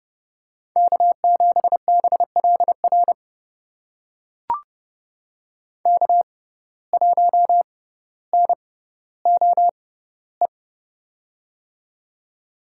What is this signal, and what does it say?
Morse code, heard as a keyed tone, about 30 wpm, spaced as for 15 wpm, 700 Hz. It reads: K7BLR K1NOE
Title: USA Call Signs - 5 Characters - 6x Character Spacing - 30wpm